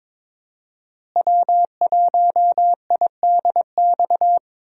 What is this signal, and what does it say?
Morse code, heard as a keyed tone, about 22 wpm, 700 Hz. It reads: W1IDX
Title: USA Call Signs - 5 Characters - Repeated 2x - 22wpm